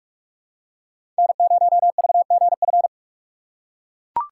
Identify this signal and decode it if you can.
Morse code, heard as a keyed tone, about 42 wpm, 690 Hz. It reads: N0VGF
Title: USA Call Signs - 5 Characters - 6x Character Spacing - 45wpm